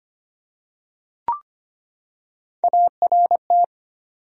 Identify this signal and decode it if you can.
Morse code, heard as a keyed tone, about 25 wpm, 700 Hz. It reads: ART
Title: USA Common Names - Max 4 Letters - 25wpm